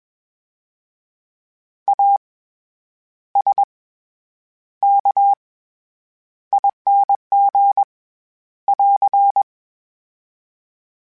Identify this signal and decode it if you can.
Morse code, heard as a keyed tone, about 21 wpm, 800 Hz. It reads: A S K ING <AR>
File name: Special INGs-mp3s / Special INGs-mp3s / SlowToFaster ing @20